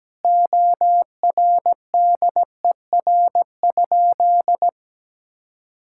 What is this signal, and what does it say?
Morse code, heard as a keyed tone, about 17 wpm, 700 Hz. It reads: ORDER?
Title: Sentences from Top 1000 Words - Review Part 6 - 17wpm